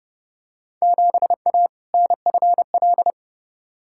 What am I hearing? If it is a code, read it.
Morse code, heard as a keyed tone, about 30 wpm, 700 Hz. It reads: 7A NFL
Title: Field Day Exchanges Encore - 30wpm